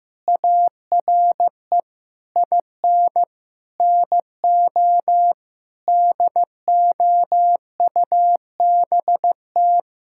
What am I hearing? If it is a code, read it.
Morse code, heard as a keyed tone, about 16 wpm, 700 Hz. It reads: ARE IN NO DOUBT
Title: Sentences from Top 1600 Words 15wpm